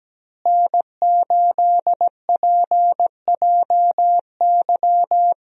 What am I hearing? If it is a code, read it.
Morse code, heard as a keyed tone, about 17 wpm, 700 Hz. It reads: N8PJY
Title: USA Call Signs - 5 Characters - Repeated 3x - 17wpm